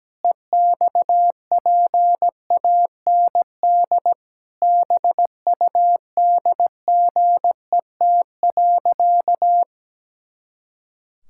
Morse code, heard as a keyed tone, about 17 wpm, 700 Hz. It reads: EXPAND BUDGET.
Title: Pangram Challenge v2 - 17wpm